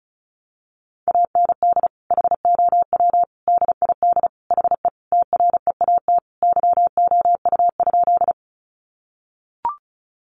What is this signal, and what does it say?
Morse code, heard as a keyed tone, about 34 wpm, 700 Hz. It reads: AND HOW DID HE TREAT YOU?
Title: Sentences from Top 1300 Words 35wpm